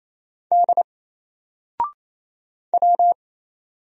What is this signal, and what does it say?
Morse code, heard as a keyed tone, about 27 wpm, 700 Hz. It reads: D W
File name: Single-Letter-Number-28wpm